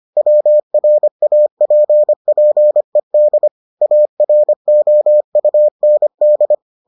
Morse code, heard as a keyed tone, about 25 wpm, 600 Hz. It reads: WRAPPED AROUND